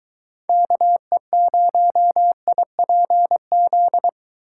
Morse code, heard as a keyed tone, about 23 wpm, 700 Hz. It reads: KE0IPZ